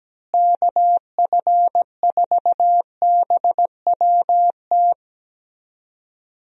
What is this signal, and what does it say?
Morse code, heard as a keyed tone, about 17 wpm, 700 Hz. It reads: KF4BWT
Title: USA Call Signs - 6 Characters - Repeated 3x - 17wpm